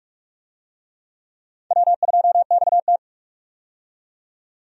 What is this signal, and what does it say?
Morse code, heard as a keyed tone, about 46 wpm, 700 Hz. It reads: W2XT
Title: USA Call Signs - 4 Characters - 45wpm